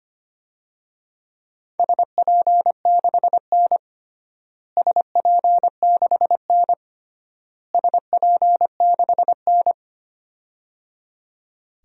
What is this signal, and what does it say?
Morse code, heard as a keyed tone, about 25 wpm, 700 Hz. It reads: SP6N SP6N SP6N
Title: DXCC-100 Call Signs - 4 Characters - Repeated 3x - 25wpm